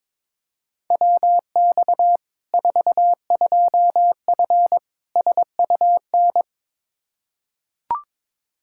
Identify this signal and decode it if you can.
Morse code, heard as a keyed tone, about 22 wpm, 700 Hz. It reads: WX 42F SUN
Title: K1USN SST QSO Practice 22wpm